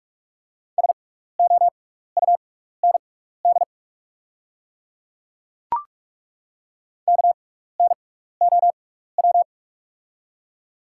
Morse code, heard as a keyed tone, about 43 wpm, spaced as for 15 wpm, 690 Hz. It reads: SOUND KNOW
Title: 6x Character Spacing - Top 100 Words - 45wpm